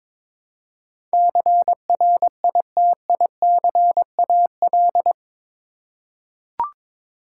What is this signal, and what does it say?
Morse code, heard as a keyed tone, about 21 wpm, 700 Hz. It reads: CRITICAL
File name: practice / 22wpm / Eight Letter Words - 22wpm